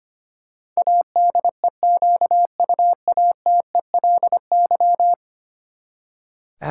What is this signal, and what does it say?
Morse code, heard as a keyed tone, about 25 wpm, 700 Hz. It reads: ADEQUATELY
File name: Ten Letter Words - 25wpm